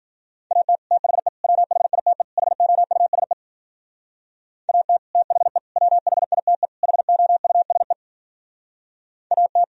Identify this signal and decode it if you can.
Morse code, heard as a keyed tone, about 46 wpm, 700 Hz. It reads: AT THE WHITE HOUSE AT THE WHITE HOUSE AT
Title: Speed-Racing - Sets of 4 Words 40wpm